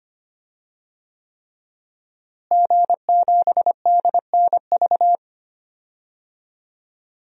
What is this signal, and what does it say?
Morse code, heard as a keyed tone, about 25 wpm, 700 Hz. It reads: G7DNV
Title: DXCC-100 Call Signs - 5 Characters - Repeated 3x - 25wpm